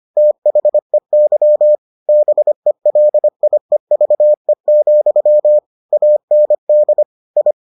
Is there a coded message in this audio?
Yes: Morse code from 0.2 to 7.5 s, heard as a keyed tone, about 25 wpm, 600 Hz. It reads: THEY BELIEVE, AND I